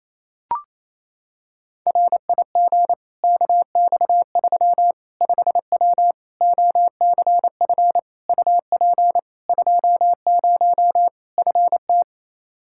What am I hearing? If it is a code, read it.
Morse code, heard as a keyed tone, about 28 wpm, 700 Hz. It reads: RIG KX3 5W OCF UP 20 FT
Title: Practice QSO - Exchange 2 - Info - 28wpm